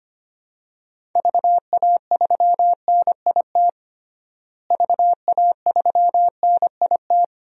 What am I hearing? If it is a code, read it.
Morse code, heard as a keyed tone, about 25 wpm, 700 Hz. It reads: VA3NIT VA3NIT
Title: DXCC-100 Call Signs - 6 Characters - Repeated 2x - 25wpm